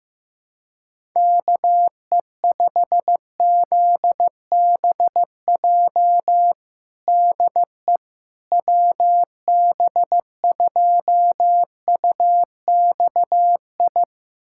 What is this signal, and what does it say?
Morse code, heard as a keyed tone, about 15 wpm, 700 Hz. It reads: KE5ZBJ DE WB2UXI
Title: USA Callsign DE Callsign 15wpm